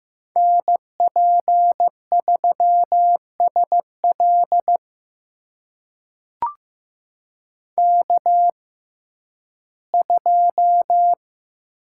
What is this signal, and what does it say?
Morse code, heard as a keyed tone, about 15 wpm, spaced as for 11 wpm, 700 Hz. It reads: NP3SL K2
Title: USA Call Signs - 5 Characters - 6x Character Spacing - 15wpm